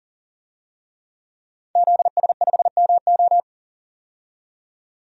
Morse code, heard as a keyed tone, about 40 wpm, 700 Hz. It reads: ZS5MO
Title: DXCC-100 Call Signs - 5 Characters - 40wpm